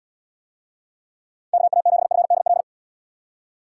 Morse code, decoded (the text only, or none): KA7FDL